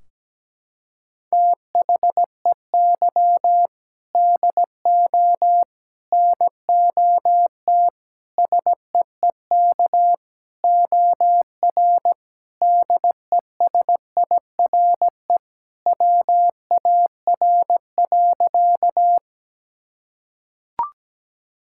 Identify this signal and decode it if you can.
Morse code, heard as a keyed tone, about 17 wpm, 700 Hz. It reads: THEY DO NOT SEEK OR DESIRE WAR.
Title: Sentences from Top 1600 Words 17wpm